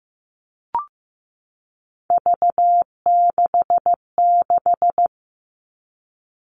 Morse code, heard as a keyed tone, about 15 wpm, 700 Hz. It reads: V66
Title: All Cut Numbers - 3-Digit 15wpm